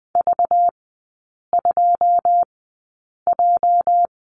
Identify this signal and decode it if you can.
Morse code, heard as a keyed tone, about 20 wpm, spaced as for 8 wpm, 690 Hz. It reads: V2J
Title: 212 Lesson 34e - Number 8 - Three Characters 20wpm